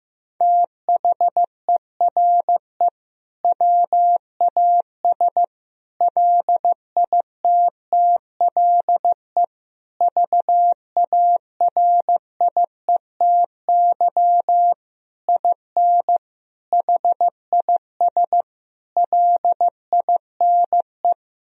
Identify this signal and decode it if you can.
Morse code, heard as a keyed tone, about 16 wpm, 700 Hz. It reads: THERE WAS LITTLE VARIETY IN HIS LINE
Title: Sentences from Top 1000 Words - Review Part 4 - 15wpm